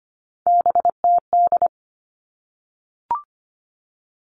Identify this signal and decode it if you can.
Morse code, heard as a keyed tone, about 25 wpm, 700 Hz. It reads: BTD
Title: All Cut Numbers - 3-Digit 25wpm